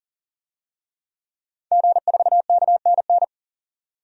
Morse code, heard as a keyed tone, about 40 wpm, 700 Hz. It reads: G4KNN
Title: DXCC-100 Call Signs - 5 Characters - 40wpm